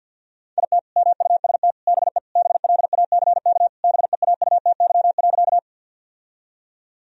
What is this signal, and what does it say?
Morse code, heard as a keyed tone, about 51 wpm, 700 Hz. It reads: IT MUST BE BLACK BEAUTY.
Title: Sentences from Top 1000 Words - Review Part 2 - 50wpm